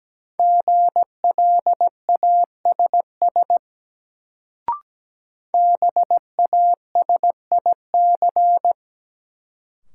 Morse code, heard as a keyed tone, about 17 wpm, 700 Hz. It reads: GLASS BASIC
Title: Sentences from Top 1000 Words - Review Part 3 - New Word Drill - 17wpm